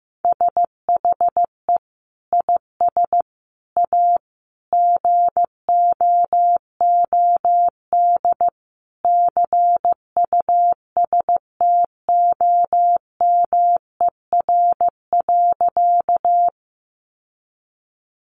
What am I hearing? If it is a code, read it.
Morse code, heard as a keyed tone, about 15 wpm, 700 Hz. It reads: SHE IS A GOOD CUSTOMER.
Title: Sentences from Top 1100 Words 15wpm